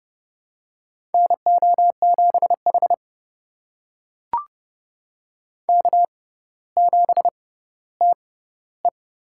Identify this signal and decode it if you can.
Morse code, heard as a keyed tone, about 30 wpm, spaced as for 15 wpm, 700 Hz. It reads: NO7H K7TE
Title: USA Call Signs - 6x Character Spacing - 30wpm